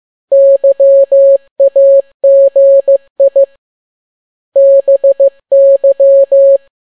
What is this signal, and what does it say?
Morse code, heard as a keyed tone, about 15 wpm, 550 Hz. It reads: YAGI BY